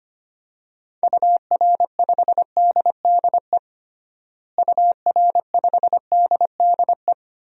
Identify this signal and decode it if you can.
Morse code, heard as a keyed tone, about 25 wpm, 700 Hz. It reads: UR5DDE UR5DDE
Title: DXCC-100 Call Signs - 6 Characters - Repeated 3x - 25wpm